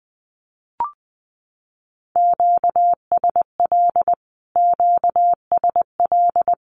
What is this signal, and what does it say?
Morse code, heard as a keyed tone, about 20 wpm, 700 Hz. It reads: QSL QSL
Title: Q-Codes Repeated 3x 20wpm